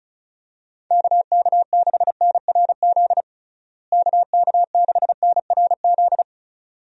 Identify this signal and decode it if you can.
Morse code, heard as a keyed tone, about 36 wpm, 700 Hz. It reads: KK6NRZ KK6NRZ
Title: USA Call Signs - 6 Characters - Repeated 2x - 35wpm